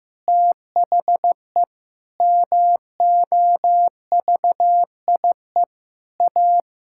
Morse code, heard as a keyed tone, about 15 wpm, 700 Hz. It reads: THE MOVIE A
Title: Sentences from Top 1000 Words - Review Part 6 - 15wpm